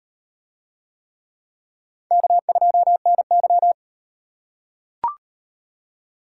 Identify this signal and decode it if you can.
Morse code, heard as a keyed tone, about 38 wpm, 700 Hz. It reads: K2NY